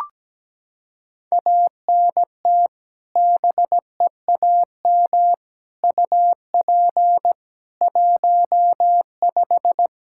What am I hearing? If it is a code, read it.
Morse code, heard as a keyed tone, about 17 wpm, 700 Hz. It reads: ANT BEAM UP 15
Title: Practice QSO - Exchange 2 - Info - 17wpm